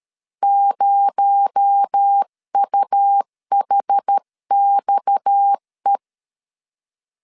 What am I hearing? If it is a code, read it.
Morse code, heard as a keyed tone, about 13 wpm, 800 Hz. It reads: 0UHXE
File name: CW / DET / 1 6 1500C S C 12wpm 3533